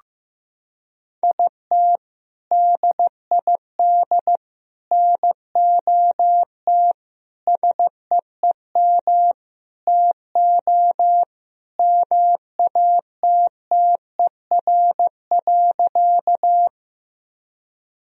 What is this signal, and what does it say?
Morse code, heard as a keyed tone, about 15 wpm, 700 Hz. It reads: IT DID NOT SEEM TO MATTER.
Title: Sentences from Top 1000 Words - Review Part 1 - 15wpm